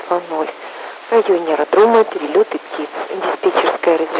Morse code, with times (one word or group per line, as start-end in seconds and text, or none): none